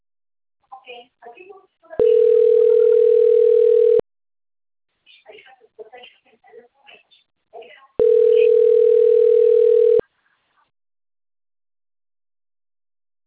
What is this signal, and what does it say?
Ringback tone (North America), 440 + 480 Hz, with speech.
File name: out- - - - - .18198